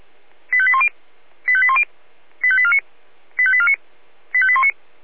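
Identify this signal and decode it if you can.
Five-tone selective call (ZVEI), heard as quick runs of five tones, tones of about 70 ms. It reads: 85810 85810 85849 85849 85819
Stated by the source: Five-tone ZVEI selective call received on an SDR.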